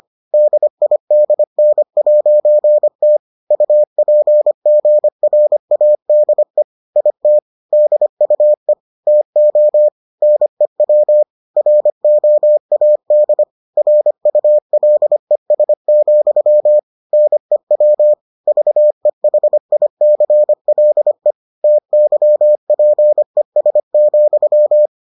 Morse code, heard as a keyed tone, about 25 wpm, 600 Hz. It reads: DIDN'T UPGRADE IT DUE TO NEW ROAD RULES, NEW VEHICLE TYPES,